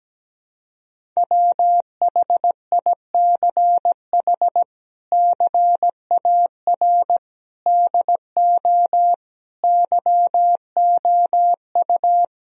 Morse code, heard as a keyed tone, about 17 wpm, 700 Hz. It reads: WHICH CAR DO YOU
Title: Sentences from Top 1000 Words - Review Part 1 - 17wpm